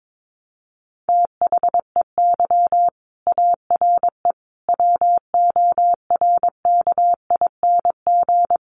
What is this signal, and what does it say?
Morse code, heard as a keyed tone, about 22 wpm, 700 Hz. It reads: THEY ARE WORKING
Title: Sentences from Top 700 Words 22wpm